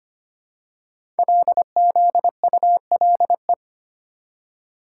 Morse code, heard as a keyed tone, about 25 wpm, 700 Hz. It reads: LZULE
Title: Code Groups - 5 Letters - 25wpm